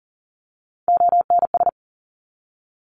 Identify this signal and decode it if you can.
Morse code, heard as a keyed tone, about 40 wpm, 700 Hz. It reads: ONS
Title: ARRL Sections 40wpm